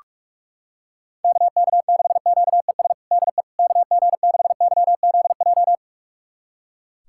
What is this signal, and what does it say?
Morse code, heard as a keyed tone, about 41 wpm, 700 Hz. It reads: KK6QES DE KG6YZJ
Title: USA Callsign DE Callsign - 6 Characters - 45wpm